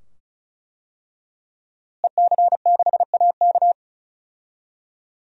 Morse code, heard as a keyed tone, about 34 wpm, 700 Hz. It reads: EC6AK
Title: DXCC-100 Call Signs - 5 Characters - Repeated 2x - 35wpm